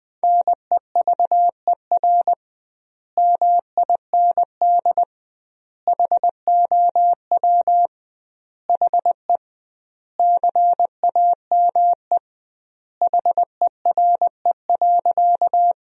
Morse code, heard as a keyed tone, about 20 wpm, 700 Hz. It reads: NEVER MIND HOW HE CAME HERE.
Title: 2x Word-Spacing - Sentences from Top 500 Words - 20wpm